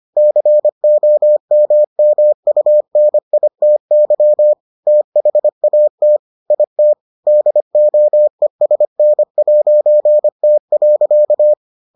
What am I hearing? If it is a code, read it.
Morse code, heard as a keyed tone, about 25 wpm, 600 Hz. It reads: COMMUNITY THAT IT DOESN'T.